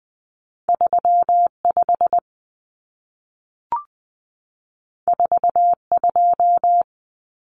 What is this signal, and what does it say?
Morse code, heard as a keyed tone, about 20 wpm, 700 Hz. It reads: 35 42
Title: Common Cut Numbers - 2-Digit 20wpm